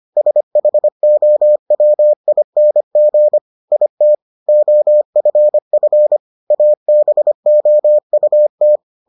Morse code, heard as a keyed tone, about 25 wpm, 600 Hz. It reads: SHOWING IT OFF ABOUT